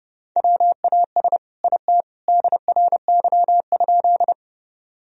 Morse code, heard as a keyed tone, about 32 wpm, 700 Hz. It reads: WAS IT DRY?
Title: Sentences from Top 1000 Words - Review Part 3 - 30wpm